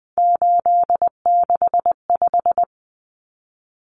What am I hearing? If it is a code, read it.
Morse code, heard as a keyed tone, about 20 wpm, 700 Hz. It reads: 865
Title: Common Cut Numbers - 3-Digit 20wpm